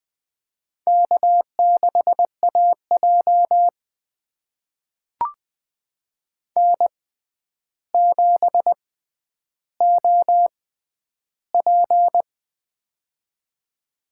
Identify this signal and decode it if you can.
Morse code, heard as a keyed tone, about 20 wpm, spaced as for 10 wpm, 700 Hz. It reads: K6AJ N7OP
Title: USA Call Signs - 4 Characters - 6x Character Spacing - 20wpm